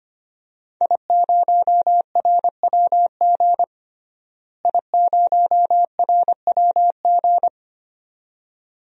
Morse code, heard as a keyed tone, about 25 wpm, 700 Hz. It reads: I0RWG I0RWG
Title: DXCC-100 Call Signs - 5 Characters - Repeated 2x - 25wpm